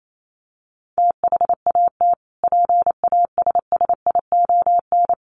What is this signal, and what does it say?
Morse code, heard as a keyed tone, about 28 wpm, 700 Hz. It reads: THAT PASSION
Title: Sentences from Top 1300 Words 28wpm